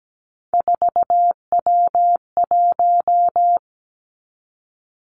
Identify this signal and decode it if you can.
Morse code, heard as a keyed tone, about 17 wpm, 700 Hz. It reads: 4W1
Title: Three Challenging Letter-Number Combinations 17wpm